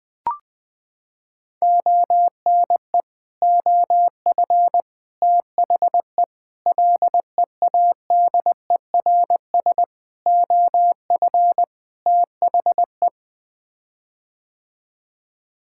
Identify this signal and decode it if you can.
Morse code, heard as a keyed tone, about 20 wpm, 700 Hz. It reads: ONE OF THE LEADERS OF THE
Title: Sets of 6 Words - 20wpm